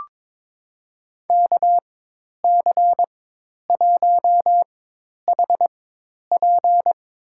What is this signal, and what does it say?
Morse code, heard as a keyed tone, about 22 wpm, spaced as for 10 wpm, 700 Hz. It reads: KC1HP
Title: USA Call Signs - 4x Character Spacing - 22wpm